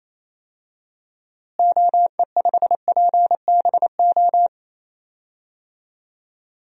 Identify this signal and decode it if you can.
Morse code, heard as a keyed tone, about 28 wpm, 700 Hz. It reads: OE5PBO
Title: DXCC-100 Call Signs - 6 Characters - Repeated 2x - 28wpm